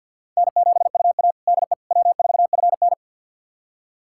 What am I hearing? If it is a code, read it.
Morse code, heard as a keyed tone, about 52 wpm, 700 Hz. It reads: N7UA DE W4FN
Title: USA Callsign DE Callsign - 4 Characters - 50wpm